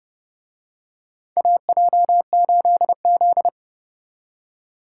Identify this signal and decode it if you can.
Morse code, heard as a keyed tone, about 29 wpm, 700 Hz. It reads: AJ8Z